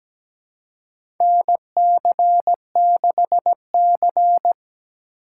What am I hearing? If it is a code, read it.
Morse code, heard as a keyed tone, about 17 wpm, 700 Hz. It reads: NC6C
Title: USA Call Signs - 4 Characters - 3x Character Spacing - 17wpm